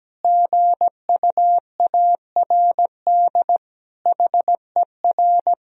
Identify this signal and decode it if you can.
Morse code, heard as a keyed tone, about 17 wpm, 700 Hz. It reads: GUARD HER
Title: Sentences from Top 1000 Words - Review Part 4 - 17wpm